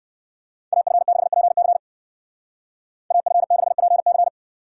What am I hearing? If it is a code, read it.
Morse code, heard as a keyed tone, about 59 wpm, 700 Hz. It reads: AF6PC AF6PC